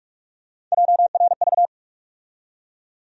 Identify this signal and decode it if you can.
Morse code, heard as a keyed tone, about 45 wpm, 700 Hz. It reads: JRV
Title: USA Call Sign Postfixes - 3 Letters 45wpm